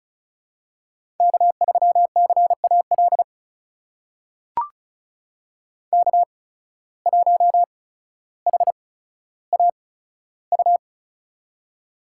Morse code, heard as a keyed tone, about 36 wpm, spaced as for 15 wpm, 700 Hz. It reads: K3CAL K1HAU
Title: USA Call Signs - 5 Characters - 8x Character Spacing - 35wpm